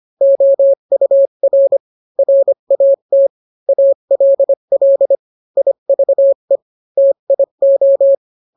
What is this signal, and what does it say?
Morse code, heard as a keyed tone, about 25 wpm, 550 Hz. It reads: OUR RAT ALL IVE TIO